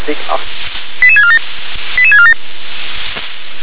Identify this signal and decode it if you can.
Five-tone selective call (ZVEI), heard as quick runs of five tones, tones of about 70 ms, with speech. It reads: 88647 88647